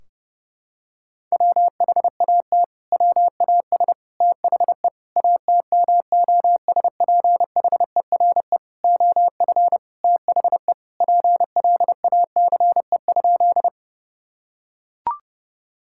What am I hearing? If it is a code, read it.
Morse code, heard as a keyed tone, about 32 wpm, 700 Hz. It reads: WHAT WAS THE ATMOSPHERE OF THE PLACE?
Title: Sentences from Top 1600 Words 30wpm